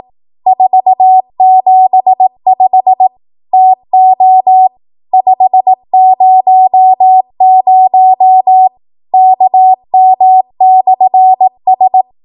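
Morse code, heard as a keyed tone, about 18 wpm, 750 Hz. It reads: 475 TO 500 KM/S
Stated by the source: ARRL 18 wpm Morse code practice archive transmission.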